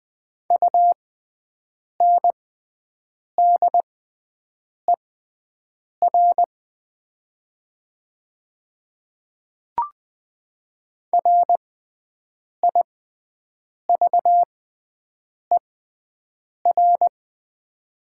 Morse code, heard as a keyed tone, about 20 wpm, spaced as for 7 wpm, 700 Hz. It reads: UNDER RIVER
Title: 6x Character Spacing - Top 500 Words - 20wpm